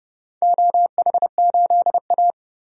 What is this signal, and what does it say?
Morse code, heard as a keyed tone, about 30 wpm, 700 Hz. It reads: OH8A
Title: DXCC-100 Call Signs - 4 Characters - Repeated 3x - 30wpm